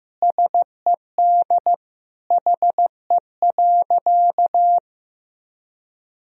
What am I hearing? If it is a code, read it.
Morse code, heard as a keyed tone, about 16 wpm, 700 Hz. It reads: SED HE.
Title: Sentences from Top 400 Words and Abbreviations 15wpm